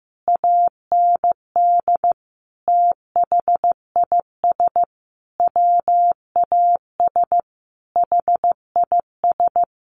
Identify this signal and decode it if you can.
Morse code, heard as a keyed tone, about 15 wpm, 700 Hz. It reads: AND THIS WAS HIS
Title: A Sound Of Thunder 15wpm